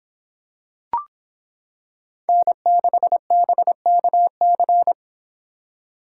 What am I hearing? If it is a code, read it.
Morse code, heard as a keyed tone, about 26 wpm, 700 Hz. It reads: N6BKC